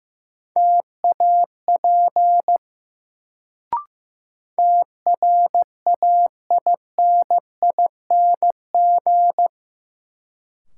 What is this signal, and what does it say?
Morse code, heard as a keyed tone, about 15 wpm, 700 Hz. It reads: TAP TRAINING
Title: Sentences from Top 1400 Words - New Word Drill Alt - 15wpm